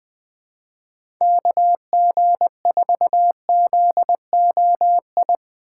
Morse code, heard as a keyed tone, about 20 wpm, 700 Hz. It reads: KG4ZOI
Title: USA Call Signs - 6 Characters - 2x Character Spacing - 20wpm